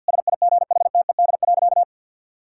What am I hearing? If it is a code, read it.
Morse code, heard as a keyed tone, about 47 wpm, 700 Hz. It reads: SIGHTED.
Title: Pangram Challenge v2 - 50wpm